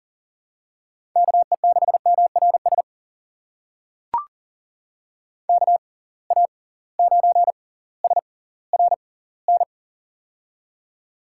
Morse code, heard as a keyed tone, about 37 wpm, spaced as for 20 wpm, 700 Hz. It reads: KE6MRS KA9SRN
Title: USA Call Signs - 6 Characters - 6x Character Spacing - 40wpm